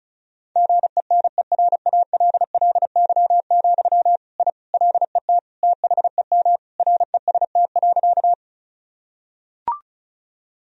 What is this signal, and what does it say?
Morse code, heard as a keyed tone, about 34 wpm, 700 Hz. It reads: GENERALLY, I LET THEM REST.
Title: Sentences from Top 1000 Words - Review Part 4 - 35wpm